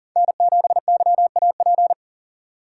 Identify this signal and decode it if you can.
Morse code, heard as a keyed tone, about 40 wpm, 700 Hz. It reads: N7YAP
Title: USA Call Signs - 5 Characters - 2x Character Spacing - 40wpm